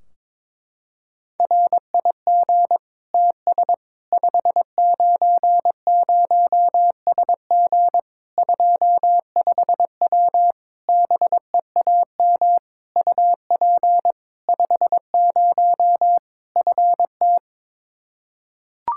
Morse code, heard as a keyed tone, about 22 wpm, 700 Hz. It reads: RIG TS 590SG 25W BEAM UP 50 FT
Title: Practice QSO - Exchange 2 - Info - 22wpm